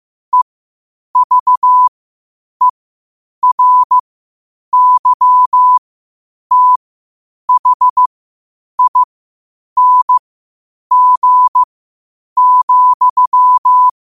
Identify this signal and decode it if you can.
Morse code, heard as a keyed tone, about 15 wpm, spaced as for 8 wpm, 1000 Hz. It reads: EVERYTHING,
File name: qotdfiles / 8wpm